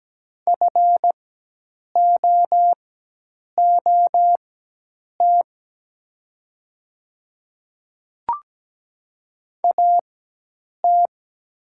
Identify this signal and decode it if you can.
Morse code, heard as a keyed tone, about 17 wpm, spaced as for 8 wpm, 700 Hz. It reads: FOOT AT
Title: 4x Character Spacing - Top 500 Words - 17wpm